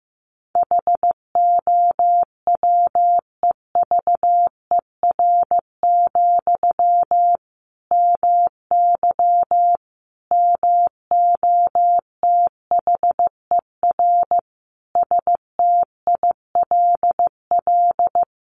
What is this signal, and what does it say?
Morse code, heard as a keyed tone, about 15 wpm, 700 Hz. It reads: HOWEVER, MY MOTHER STILL
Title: Sentences from Top 1000 Words 15wpm